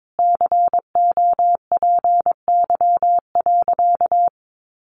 Morse code, heard as a keyed tone, about 22 wpm, 700 Hz. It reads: COPY.
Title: Sentences from Top 1400 Words 22wpm